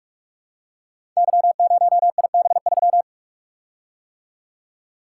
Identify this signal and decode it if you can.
Morse code, heard as a keyed tone, about 44 wpm, 700 Hz. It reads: Y0IB3
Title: Code Groups - 5 Letter:Numbers - 45wpm